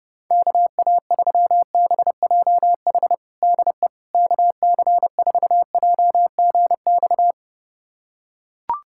Morse code, heard as a keyed tone, about 30 wpm, 700 Hz. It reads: KA3BJH DE KC4JGX
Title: USA Callsign DE Callsign - 6 Characters - 30wpm